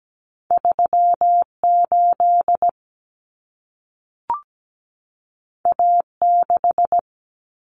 Morse code, heard as a keyed tone, about 17 wpm, 700 Hz. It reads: 38 A6
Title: Common Cut Numbers - 2-Digit 17wpm